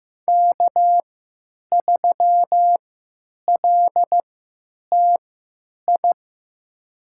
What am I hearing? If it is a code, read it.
Morse code, heard as a keyed tone, about 15 wpm, spaced as for 8 wpm, 700 Hz. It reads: K3LTI
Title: USA Call Signs - 5 Characters - 3x Character Spacing - 15wpm